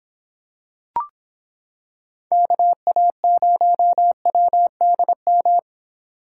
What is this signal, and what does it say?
Morse code, heard as a keyed tone, about 26 wpm, 700 Hz. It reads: KA0WDM